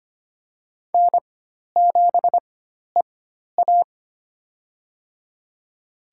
Morse code, heard as a keyed tone, about 25 wpm, spaced as for 12 wpm, 700 Hz. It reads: N7EA
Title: USA Call Signs - 4 Characters - 4x Character Spacing - 25wpm